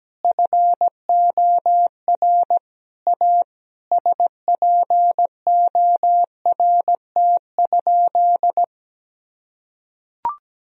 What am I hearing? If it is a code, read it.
Morse code, heard as a keyed tone, about 17 wpm, 700 Hz. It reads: FOR A SPORT?
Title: Sentences from Top 1000 Words - Review Part 4 - 17wpm